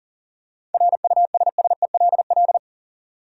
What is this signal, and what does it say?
Morse code, heard as a keyed tone, about 40 wpm, 700 Hz. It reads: RUSSELL
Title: USA Common Names - 40wpm